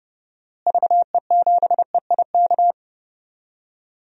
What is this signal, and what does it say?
Morse code, heard as a keyed tone, about 31 wpm, 700 Hz. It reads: VE7EIK
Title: DXCC-100 Call Signs - 6 Characters - Repeated 3x - 30wpm